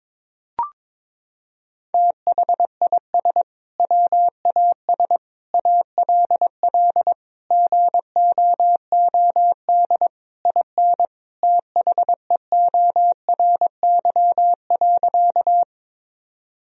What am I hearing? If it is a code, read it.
Morse code, heard as a keyed tone, about 22 wpm, 700 Hz. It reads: THIS WAS ALL GOOD IN THEORY.
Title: Sentences from Top 1000 Words - Review Part 3 - 22wpm